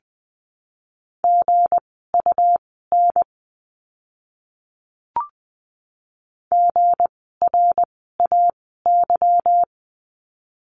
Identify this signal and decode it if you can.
Morse code, heard as a keyed tone, about 20 wpm, spaced as for 14 wpm, 700 Hz. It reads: GUN GRAY
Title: 160 Lesson 26f - Letter g - Words 20wpm